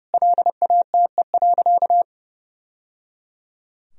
Morse code, heard as a keyed tone, about 30 wpm, 700 Hz. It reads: LATE.
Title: Sentences from Top 1000 Words - Review Part 4 - 30wpm